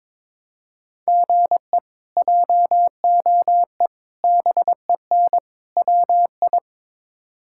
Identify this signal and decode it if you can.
Morse code, heard as a keyed tone, about 22 wpm, 700 Hz. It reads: GE JOE BEN WI
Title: K1USN SST QSO Practice - 22wpm